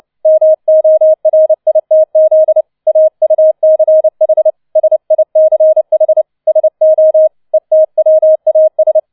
Morse code, heard as a keyed tone, about 29 wpm, 630 Hz. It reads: MORITZ AUCH SICH SO ETWAS
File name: 006 Sechster Streich 30wpm